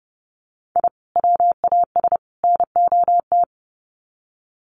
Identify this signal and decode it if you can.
Morse code, heard as a keyed tone, about 30 wpm, 700 Hz. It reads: I WAS NOT
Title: Sets of 3 Words 30wpm